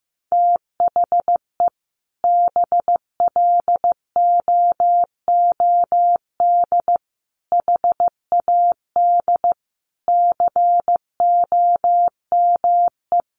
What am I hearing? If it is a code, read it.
Morse code, heard as a keyed tone, about 15 wpm, 700 Hz. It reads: THE BLOOD HAD COME
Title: Sentences from Top 1300 Words 15wpm